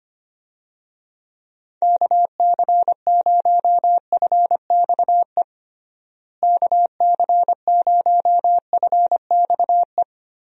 Morse code, heard as a keyed tone, about 25 wpm, 700 Hz. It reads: KC0FXE KC0FXE
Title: USA Call Signs - 6 Characters - Repeated 2x - 25wpm